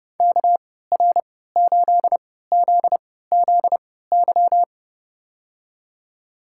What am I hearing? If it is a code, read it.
Morse code, heard as a keyed tone, about 30 wpm, spaced as for 17 wpm, 700 Hz. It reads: KR8ZZY
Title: USA Call Signs - 3x Character Spacing - 30wpm